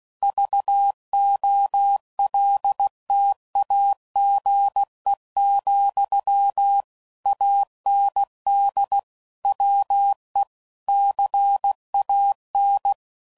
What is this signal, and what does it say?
Morse code, heard as a keyed tone, about 16 wpm, 800 Hz. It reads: VOLTAGE, AND WE CAN
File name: CW Code Groups @15WPM/Technical Subjects